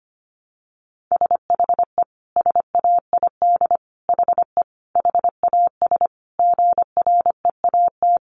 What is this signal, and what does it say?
Morse code, heard as a keyed tone, about 25 wpm, 700 Hz. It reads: SHE SAID HE HAS GREAT
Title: Sentences from Top 1300 Words 25wpm